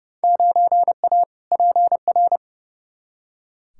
Morse code, heard as a keyed tone, about 30 wpm, 700 Hz. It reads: 9A PR
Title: Field Day Exchanges Encore - 30wpm